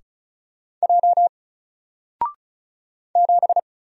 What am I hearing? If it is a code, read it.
Morse code, heard as a keyed tone, about 35 wpm, 700 Hz. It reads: J 7
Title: Single Letter-Number 35wpm